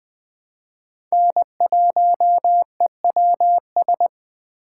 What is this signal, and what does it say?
Morse code, heard as a keyed tone, about 20 wpm, 700 Hz. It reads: N1EWS
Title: USA Call Signs - 5 Characters - 4x Character Spacing - 20wpm